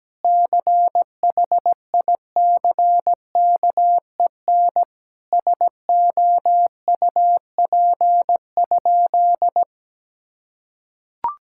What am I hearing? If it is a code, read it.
Morse code, heard as a keyed tone, about 17 wpm, 700 Hz. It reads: CHICKEN SOUP?
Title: Sentences from Top 1000 Words - Review Part 6 - 17wpm